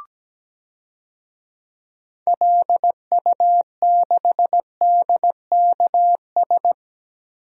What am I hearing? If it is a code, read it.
Morse code, heard as a keyed tone, about 17 wpm, 700 Hz. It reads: LU6DKS
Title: DXCC-100 Call Signs - 6 Characters - Repeated 3x - 17wpm